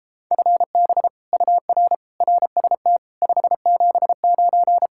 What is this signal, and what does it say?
Morse code, heard as a keyed tone, about 29 wpm, 700 Hz. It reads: FB UR RST 579